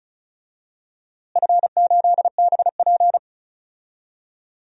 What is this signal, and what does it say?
Morse code, heard as a keyed tone, about 34 wpm, 700 Hz. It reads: F8BP
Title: DXCC-100 Call Signs - 4 Characters - Repeated 2x - 35wpm